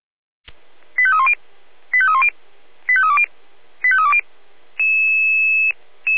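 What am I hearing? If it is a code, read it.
Five-tone selective call (ZVEI), heard as quick runs of five tones, tones of about 70 ms. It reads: 85310 85310 85320 85320
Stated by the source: ZVEI five-tone selective call 85340 received by SDR.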